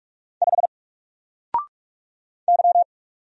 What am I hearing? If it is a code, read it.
Morse code, heard as a keyed tone, about 44 wpm, 700 Hz. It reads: 5 Y
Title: Single Letter-Number Alt 45wpm